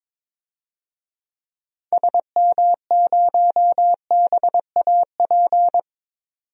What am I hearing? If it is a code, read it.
Morse code, heard as a keyed tone, about 21 wpm, 700 Hz. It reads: SM0BAP
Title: DXCC-100 Call Signs - 6 Characters - 22wpm